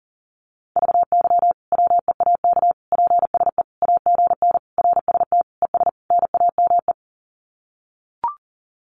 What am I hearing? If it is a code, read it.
Morse code, heard as a keyed tone, about 37 wpm, 700 Hz. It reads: VY WEAK PSE AGN RST ES NAME
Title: Common QSO Phrases 40wpm